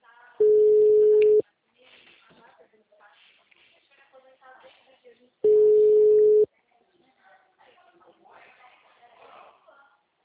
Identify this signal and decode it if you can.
Ringback tone (Europe, 425 Hz), with speech.